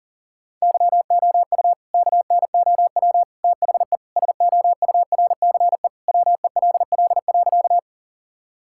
Morse code, heard as a keyed tone, about 40 wpm, 700 Hz. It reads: YOU KNOW THE SOURCE WELL.
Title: Sentences from Top 1000 Words - Review Part 5 - 40wpm